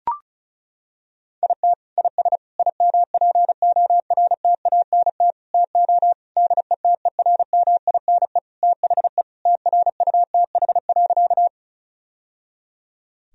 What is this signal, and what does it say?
Morse code, heard as a keyed tone, about 36 wpm, 700 Hz. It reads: IT IS IMPORTANT TO DETERMINE THE TRUTH.
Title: Sentences from Top 1000 Words - Review Part 4 - 35wpm